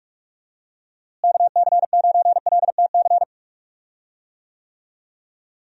Morse code, heard as a keyed tone, about 44 wpm, 700 Hz. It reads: KC9LTC
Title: USA Call Signs - 6 Characters - Repeated 2x - 45wpm